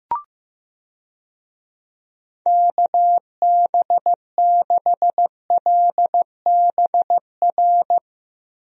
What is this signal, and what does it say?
Morse code, heard as a keyed tone, about 15 wpm, 700 Hz. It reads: KB6LBR
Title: USA Call Signs - 6 Characters - Repeated 2x - 15wpm